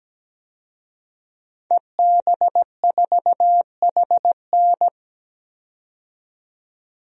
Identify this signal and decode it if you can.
Morse code, heard as a keyed tone, about 17 wpm, 700 Hz. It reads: EB4HN
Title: DXCC-100 Call Signs - 5 Characters - 17wpm